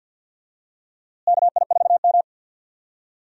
Morse code, heard as a keyed tone, about 51 wpm, 700 Hz. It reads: KI4M
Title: USA Call Signs - 4 Characters - Repeated 2x - 50wpm